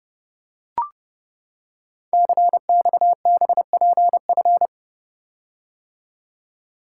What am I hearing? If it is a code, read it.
Morse code, heard as a keyed tone, about 30 wpm, 700 Hz. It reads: CXBPF
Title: Code Groups - 5 Letters - 30wpm